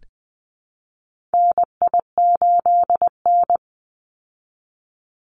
Morse code, heard as a keyed tone, about 20 wpm, 700 Hz. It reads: NI8N
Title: Call Signs Repeated 2x - 20wpm